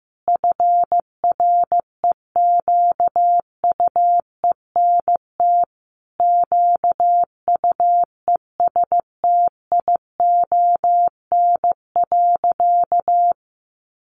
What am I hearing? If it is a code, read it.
Morse code, heard as a keyed tone, about 15 wpm, 700 Hz. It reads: FREQUENT QUESTION.
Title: Sentences from Top 1400 Words 15wpm